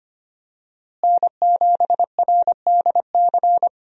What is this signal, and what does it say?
Morse code, heard as a keyed tone, about 25 wpm, 700 Hz. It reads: N7RDC
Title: USA Call Signs - 5 Characters - Repeated 2x - 25wpm